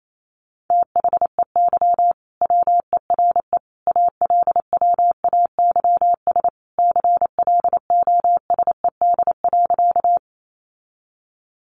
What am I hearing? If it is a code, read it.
Morse code, heard as a keyed tone, about 28 wpm, 700 Hz. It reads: THEY WERE ALWAYS CLOSED.
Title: Sentences from Top 1400 Words 28wpm